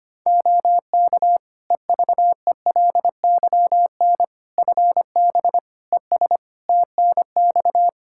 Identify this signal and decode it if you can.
Morse code, heard as a keyed tone, about 25 wpm, 700 Hz. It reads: OK EVELYN FB ES TNX
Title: Practice QSO - Exchange 2 - Info - 25wpm